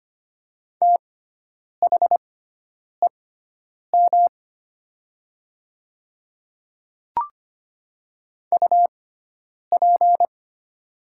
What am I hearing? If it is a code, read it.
Morse code, heard as a keyed tone, about 25 wpm, spaced as for 9 wpm, 700 Hz. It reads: THEM UP
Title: 6x Character Spacing - Top 100 Words - 25wpm